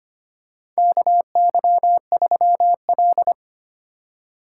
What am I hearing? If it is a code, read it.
Morse code, heard as a keyed tone, about 25 wpm, 700 Hz. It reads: KY3L
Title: USA Call Signs - 4 Characters - 4x Character Spacing - 25wpm